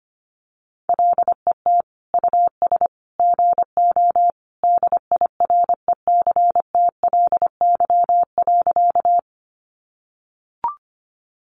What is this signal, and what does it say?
Morse code, heard as a keyed tone, about 25 wpm, 700 Hz. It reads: LET US GO DIRECTLY.
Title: Sentences from Top 800 Words 25wpm